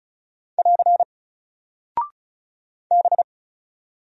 Morse code, heard as a keyed tone, about 36 wpm, 700 Hz. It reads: <AR> B
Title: Single Character (Letter:Number:Punctuation:Prosign) Alt 35wpm